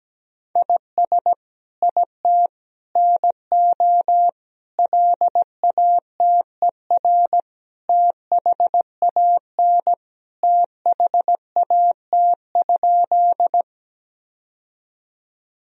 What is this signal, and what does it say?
Morse code, heard as a keyed tone, about 17 wpm, 700 Hz. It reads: IS IT NO LATER THAN THAT?
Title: Sentences from Top 1000 Words - Review Part 4 - 17wpm